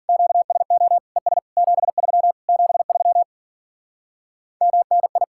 Morse code, heard as a keyed tone, about 50 wpm, 700 Hz. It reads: QSO ES 73 73 MNI
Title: Speed-Racing Common QSO Phrases Encore - 40wpm